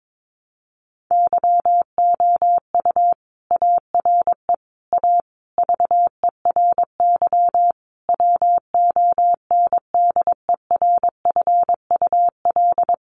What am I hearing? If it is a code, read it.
Morse code, heard as a keyed tone, about 22 wpm, 700 Hz. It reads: YOU ARE A VERY WONDERFUL
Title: Sentences from Top 1100 Words 22wpm